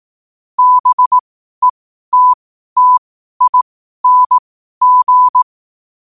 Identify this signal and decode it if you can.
Morse code, heard as a keyed tone, about 18 wpm, spaced as for 12 wpm, 1000 Hz. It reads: BETTING
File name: qotdfiles / 12wpm